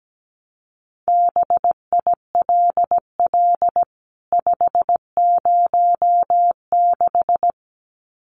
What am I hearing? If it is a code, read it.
Morse code, heard as a keyed tone, about 17 wpm, 700 Hz. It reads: BILL 506
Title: CWT Exchange 17wpm